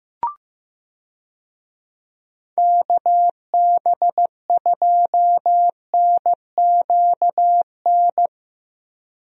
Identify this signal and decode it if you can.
Morse code, heard as a keyed tone, about 15 wpm, 700 Hz. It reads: KB2NQN
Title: USA Call Signs - 6 Characters - Repeated 2x - 15wpm